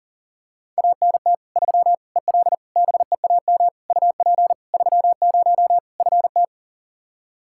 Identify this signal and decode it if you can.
Morse code, heard as a keyed tone, about 40 wpm, 700 Hz. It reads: ANT 3 EL BEAM UP 30 FT
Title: Practice QSO - Exchange 2 - Info - 40wpm